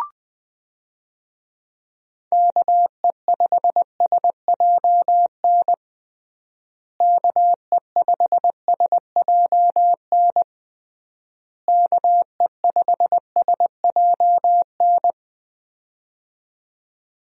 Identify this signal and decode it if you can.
Morse code, heard as a keyed tone, about 20 wpm, 700 Hz. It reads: KE5SJN KE5SJN KE5SJN
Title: USA Call Signs - 6 Characters - Repeated 3x - 20wpm